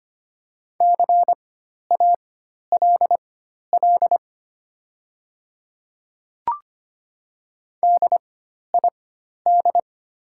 Morse code, heard as a keyed tone, about 25 wpm, spaced as for 12 wpm, 700 Hz. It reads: CALL DID
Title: 4x Character Spacing - Top 200 Words - 25wpm